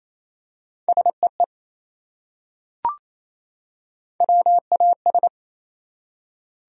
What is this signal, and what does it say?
Morse code, heard as a keyed tone, about 28 wpm, 700 Hz. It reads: SEE WAS
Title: Top 100 Words 28wpm